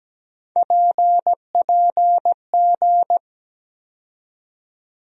Morse code, heard as a keyed tone, about 17 wpm, 700 Hz. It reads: PPG
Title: USA Call Sign Postfixes - 3 Letters 17wpm